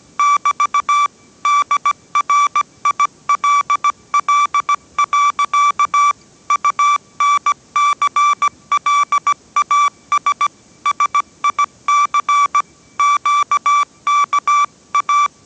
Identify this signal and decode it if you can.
Morse code, heard as a keyed tone, about 17 wpm, 1160 Hz. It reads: <BT> DRILL. UNCLAS SIC QKA
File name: 18wpm